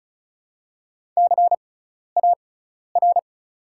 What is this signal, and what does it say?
Morse code, heard as a keyed tone, about 36 wpm, spaced as for 12 wpm, 700 Hz. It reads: CAR
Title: 6x Character Spacing - Top 500 Words - 35wpm